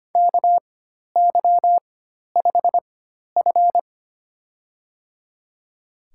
Morse code, heard as a keyed tone, about 25 wpm, spaced as for 12 wpm, 700 Hz. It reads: KY5F
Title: USA Call Signs - 4 Characters - 4x Character Spacing - 25wpm